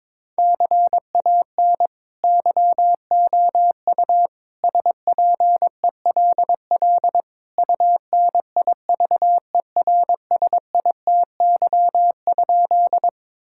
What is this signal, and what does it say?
Morse code, heard as a keyed tone, about 22 wpm, 700 Hz. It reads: CAN YOU SPELL UNIVERSITY?
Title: Sentences from Top 1000 Words - Review Part 5 - 22wpm